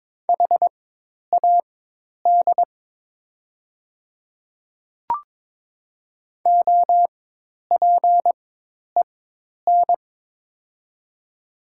Morse code, heard as a keyed tone, about 22 wpm, spaced as for 10 wpm, 700 Hz. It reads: HAD OPEN
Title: 4x Character Spacing - Top 300 Words - 22wpm